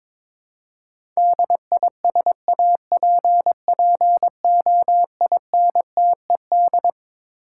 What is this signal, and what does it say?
Morse code, heard as a keyed tone, about 22 wpm, 700 Hz. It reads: DISAPPOINTED
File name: Twelve Letter Words - 22wpm